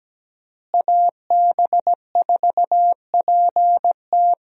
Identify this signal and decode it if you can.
Morse code, heard as a keyed tone, about 17 wpm, 700 Hz. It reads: AB4PT
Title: USA Call Signs - 5 Characters - Repeated 3x - 17wpm